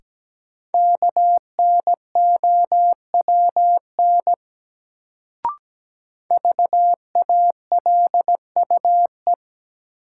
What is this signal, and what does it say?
Morse code, heard as a keyed tone, about 17 wpm, 700 Hz. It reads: KNOWN VALUE
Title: Sentences from Top 600 Words - New Word Drill - 17wpm